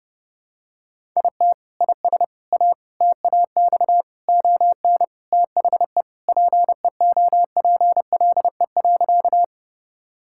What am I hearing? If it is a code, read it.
Morse code, heard as a keyed tone, about 30 wpm, 700 Hz. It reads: IT IS A TAX ON THE PEOPLE.
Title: Sentences from Top 1000 Words - Review Part 5 - 30wpm